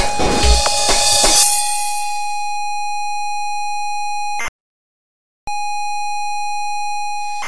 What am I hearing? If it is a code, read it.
Computer data on cassette.